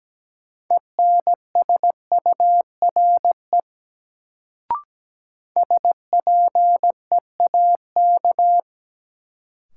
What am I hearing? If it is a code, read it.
Morse code, heard as a keyed tone, about 17 wpm, 700 Hz. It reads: ENSURE SPEAK
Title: Sentences from Top 1100 Words - New Word Drill Alt - 17wpm